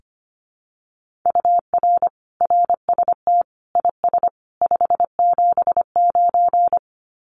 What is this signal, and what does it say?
Morse code, heard as a keyed tone, about 25 wpm, 700 Hz. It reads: UR RST IS 579
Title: Common QSO Phrases 25wpm